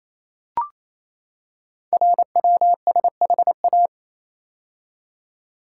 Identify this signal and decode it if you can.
Morse code, heard as a keyed tone, about 28 wpm, 700 Hz. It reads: RWSHA
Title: Code Groups - 5 Letters - 28wpm